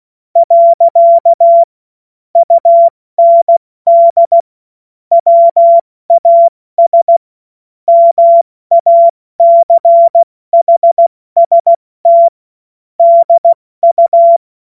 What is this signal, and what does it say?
Morse code, heard as a keyed tone, about 16 wpm, 680 Hz. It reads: . UND WAS MACHST DU
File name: Morsekurs / Witze / Witz 12. Oktober 2021, 70 F80